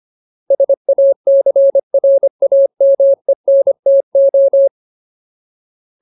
Morse code, heard as a keyed tone, about 25 wpm, 550 Hz. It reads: SACRAMENTO
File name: Ultimate Word List / 25WPM Ultimate